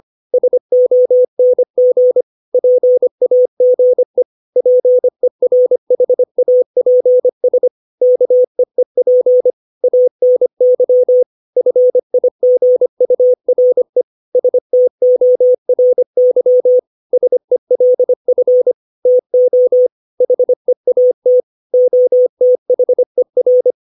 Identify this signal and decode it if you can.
Morse code, heard as a keyed tone, about 25 wpm, 500 Hz. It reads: SONG PAGE PERHAPS KEEP ANY FIGURE STORY SELF TO HEAT OTHER
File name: 632 words / 632 Words 25WPM